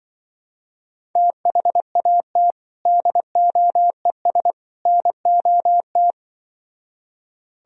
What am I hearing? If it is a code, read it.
Morse code, heard as a keyed tone, about 24 wpm, 700 Hz. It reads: THAT DOES NOT
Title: Speed-Racing - Sets of 3 Words 20wpm